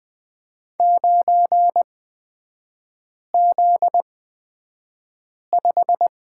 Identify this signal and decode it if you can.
Morse code, heard as a keyed tone, about 20 wpm, spaced as for 5 wpm, 700 Hz. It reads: 9Z5
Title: 243 Lesson 39e - Letter z - Three Characters 20wpm